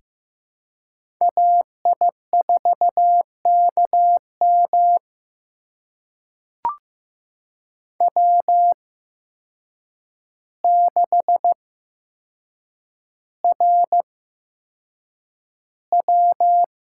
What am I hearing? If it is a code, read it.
Morse code, heard as a keyed tone, about 15 wpm, spaced as for 7 wpm, 700 Hz. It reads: AI4KM W6RW
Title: USA Call Signs - 8x Character Spacing - 15wpm